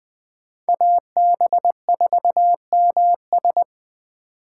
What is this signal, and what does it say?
Morse code, heard as a keyed tone, about 24 wpm, 700 Hz. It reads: AB4MS
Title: Speed Racing - USA 2x2 Call Signs 20wpm